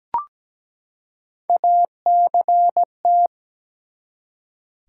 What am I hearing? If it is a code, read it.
Morse code, heard as a keyed tone, about 17 wpm, 700 Hz. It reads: ACT
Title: Three Letter Words - 17wpm